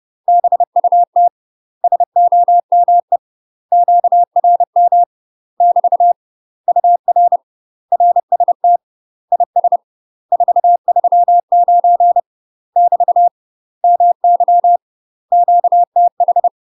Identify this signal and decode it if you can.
Morse code, heard as a keyed tone, about 30 wpm, 700 Hz. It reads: BUT SOME QRM <BT> UR RST IS 439 <BT> MY QTH
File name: UK MASTER QSOs TRAINING / 30WPM / UK MASTER QSOs 0050